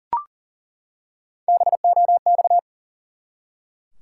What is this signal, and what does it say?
Morse code, heard as a keyed tone, about 40 wpm, 700 Hz. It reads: BOX
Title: Three Letter Words - 40wpm